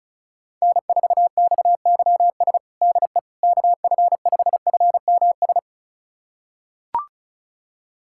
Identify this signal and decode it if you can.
Morse code, heard as a keyed tone, about 36 wpm, 700 Hz. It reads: N4XYS DE KF5FMS
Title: USA Callsign DE Callsign 35wpm